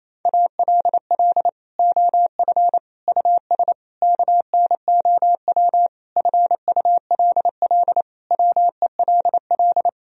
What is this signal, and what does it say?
Morse code, heard as a keyed tone, about 27 wpm, 690 Hz. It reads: ALL OF US KNOW FULL WELL
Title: Sentences from Top 1000 Words - Review Part 3 - 28wpm